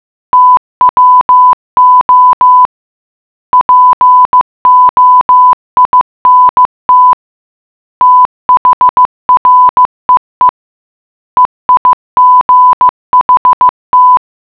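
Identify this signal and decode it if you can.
Morse code, heard as a keyed tone, about 15 wpm, 1000 Hz. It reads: TWO POINT THREE EIGHT
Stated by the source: Morse puzzle for the geocache Lambertsetersenderen - Gacrux#8 (GCRHX0) in Oslo, Norway.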